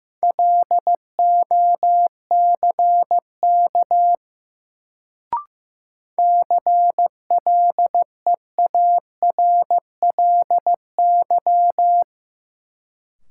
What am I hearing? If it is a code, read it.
Morse code, heard as a keyed tone, about 15 wpm, 700 Hz. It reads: LOCK CLEARLY
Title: Sentences from Top 1300 Words - New Word Drill - 15wpm